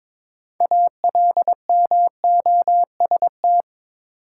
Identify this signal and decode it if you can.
Morse code, heard as a keyed tone, about 22 wpm, 700 Hz. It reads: ALMOST
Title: Sentences from Top 700 Words - New Word Drill - 22wpm